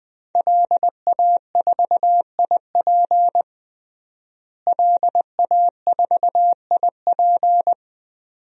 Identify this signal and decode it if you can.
Morse code, heard as a keyed tone, about 20 wpm, 700 Hz. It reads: LA4IP LA4IP
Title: DXCC-100 Call Signs - 5 Characters - Repeated 2x - 20wpm